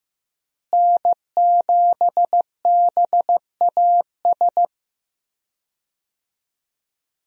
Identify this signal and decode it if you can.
Morse code, heard as a keyed tone, about 15 wpm, 700 Hz. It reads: N7BAS
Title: USA Call Signs - 5 Characters - Repeated 2x - 15wpm